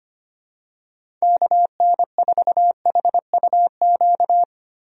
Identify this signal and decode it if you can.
Morse code, heard as a keyed tone, about 25 wpm, 700 Hz. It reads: KN4HUQ
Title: USA Call Signs - 6 Characters - 25wpm